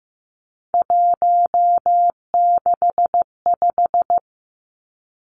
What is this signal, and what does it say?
Morse code, heard as a keyed tone, about 15 wpm, 700 Hz. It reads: 165
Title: Three Challenging Letter-Number Combinations 15wpm